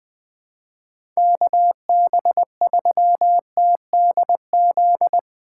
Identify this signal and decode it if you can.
Morse code, heard as a keyed tone, about 21 wpm, 700 Hz. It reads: KB3TDZ
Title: USA Call Signs - 6 Characters - 4x Character Spacing - 20wpm